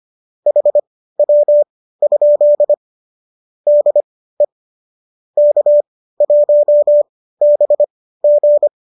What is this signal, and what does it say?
Morse code, heard as a keyed tone, about 25 wpm, spaced as for 15 wpm, 570 Hz. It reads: HW? DE K1BG